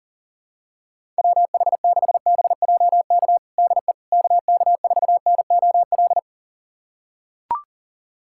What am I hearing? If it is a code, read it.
Morse code, heard as a keyed tone, about 40 wpm, 700 Hz. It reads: WH6BJK DE KK4NOL
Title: USA Callsign DE Callsign 40wpm